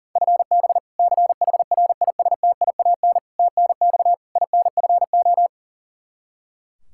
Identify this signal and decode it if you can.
Morse code, heard as a keyed tone, about 40 wpm, 700 Hz. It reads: FB CHRISTIAN TNX INFO
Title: K1USN SST QSO Practice 40wpm